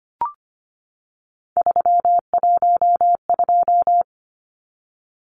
Morse code, heard as a keyed tone, about 25 wpm, 700 Hz. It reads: 312
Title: Numbers - 3-Digit 25wpm